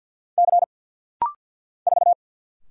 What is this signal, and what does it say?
Morse code, heard as a keyed tone, about 50 wpm, 700 Hz. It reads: C 4
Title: Single Letter-Number Alt - Rapid-Fire 50wpm